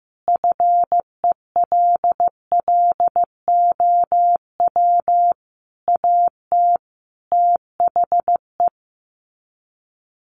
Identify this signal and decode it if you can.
Morse code, heard as a keyed tone, about 15 wpm, 700 Hz. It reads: FELLOW AT THE
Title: Sets of 5 Words Encore 15wpm